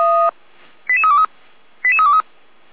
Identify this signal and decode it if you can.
Five-tone selective call (ZVEI), heard as quick runs of five tones, tones of about 70 ms. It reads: 88323 88323